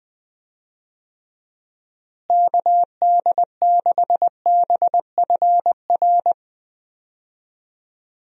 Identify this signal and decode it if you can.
Morse code, heard as a keyed tone, about 20 wpm, 700 Hz. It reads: KD6BFR
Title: USA Call Signs - 6 Characters - Repeated 2x - 20wpm